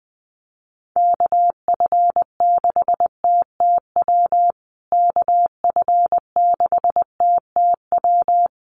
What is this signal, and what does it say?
Morse code, heard as a keyed tone, about 20 wpm, 700 Hz. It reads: KF6TTW KF6TTW
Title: USA Call Signs Repeated 3x 20wpm